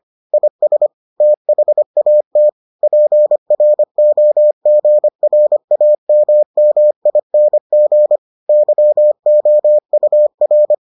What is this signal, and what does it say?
Morse code, heard as a keyed tone, about 25 wpm, 600 Hz. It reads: IS THAT PROGRAMMING YOUR